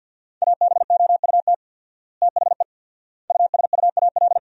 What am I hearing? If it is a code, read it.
Morse code, heard as a keyed tone, about 51 wpm, 700 Hz. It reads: ABOUT THE USUAL
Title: 4x Word-Spacing - Sentences from Top 400 Words - 50wpm